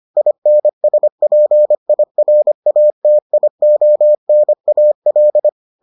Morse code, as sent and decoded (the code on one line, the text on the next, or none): .. -. ... .--. .. .-. .- - .. --- -. .- .-..
INSPIRATIONAL